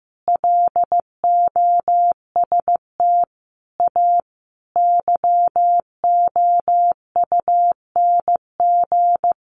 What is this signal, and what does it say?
Morse code, heard as a keyed tone, about 14 wpm, 700 Hz. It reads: LOST A YOUNG
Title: Sentences from Top 1400 Words 15wpm